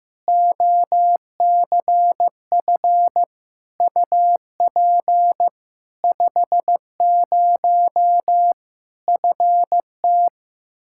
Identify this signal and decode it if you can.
Morse code, heard as a keyed tone, about 15 wpm, 700 Hz. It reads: OCF UP 50 FT
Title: Practice QSO - Exchange 2 - Info - 15wpm